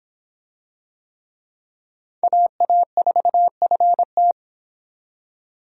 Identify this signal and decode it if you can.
Morse code, heard as a keyed tone, about 24 wpm, 700 Hz. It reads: AA4FT